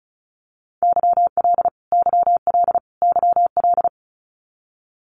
Morse code, heard as a keyed tone, about 34 wpm, 700 Hz. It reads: YL YL YL
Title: Abbreviations Repeated 3x - 35wpm